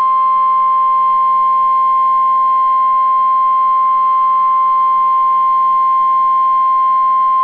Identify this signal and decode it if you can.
Emergency Alert System (SAME) tones (the 1050 Hz attention tone). It is heard from 1.2 to 6.4 s.